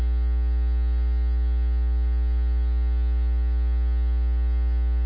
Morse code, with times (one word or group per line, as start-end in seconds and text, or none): none